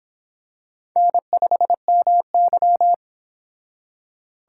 Morse code, heard as a keyed tone, about 25 wpm, 700 Hz. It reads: N5MY